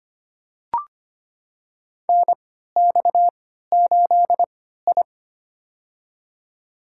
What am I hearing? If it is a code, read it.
Morse code, heard as a keyed tone, about 25 wpm, spaced as for 14 wpm, 700 Hz. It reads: NX8I
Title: USA Call Signs - 4 Characters - 3x Character Spacing - 25wpm